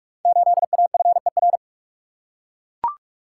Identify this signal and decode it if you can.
Morse code, heard as a keyed tone, about 44 wpm, 700 Hz. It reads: 8AFER
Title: Code Groups - 5 Letter:Numbers - 45wpm